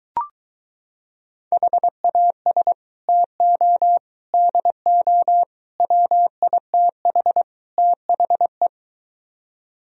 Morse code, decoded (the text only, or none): HAS TO DO WITH THE